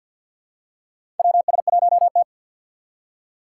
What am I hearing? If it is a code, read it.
Morse code, heard as a keyed tone, about 53 wpm, 700 Hz. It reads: WS1T
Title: USA Call Signs - 3x Character Spacing - 50wpm